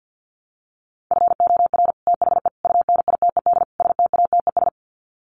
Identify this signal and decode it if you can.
Morse code, heard as a keyed tone, about 50 wpm, 700 Hz. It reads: FOR THE UNITED STATES
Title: Sets of 4 Words 50wpm